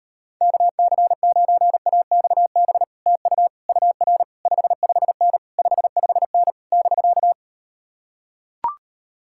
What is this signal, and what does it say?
Morse code, heard as a keyed tone, about 39 wpm, 700 Hz. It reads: KC9AXB TU UR 55N 55N <BK>